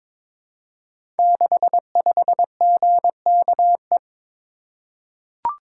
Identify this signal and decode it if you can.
Morse code, heard as a keyed tone, about 22 wpm, 700 Hz. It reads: 65GKE
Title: Code Groups - 5 Letter:Numbers - 22wpm